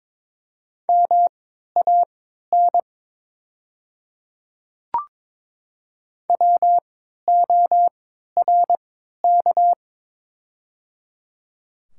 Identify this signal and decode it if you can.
Morse code, heard as a keyed tone, about 22 wpm, spaced as for 13 wpm, 690 Hz. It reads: MAN WORK
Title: 3x Character Spacing - Top 200 Words - 22wpm